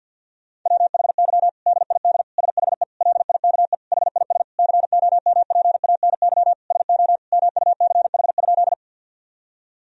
Morse code, heard as a keyed tone, about 50 wpm, 700 Hz. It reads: WHY DID SHE LIKE HIS COMPANY SO MUCH?
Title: Sentences from Top 1000 Words - Review Part 3 - 50wpm